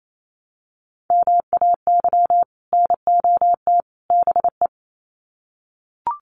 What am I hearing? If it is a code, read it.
Morse code, heard as a keyed tone, about 28 wpm, 700 Hz. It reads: MAY NOT BE
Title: Sets of 3 Words 28wpm